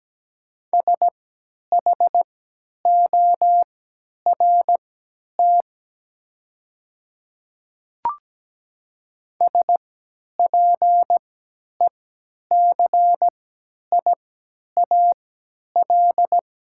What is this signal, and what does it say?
Morse code, heard as a keyed tone, about 17 wpm, spaced as for 10 wpm, 700 Hz. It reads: SHORT SPECIAL
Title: 3x Character Spacing - Top 500 Words - 17wpm